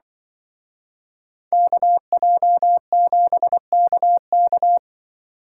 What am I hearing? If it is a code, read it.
Morse code, heard as a keyed tone, about 25 wpm, 700 Hz. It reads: KJ7KK
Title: Speed Racing - USA 2x2 Call Signs 20wpm